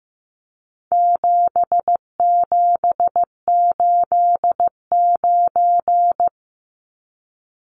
Morse code, heard as a keyed tone, about 15 wpm, 700 Hz. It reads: 7789
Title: Numbers - 4-Digit 15wpm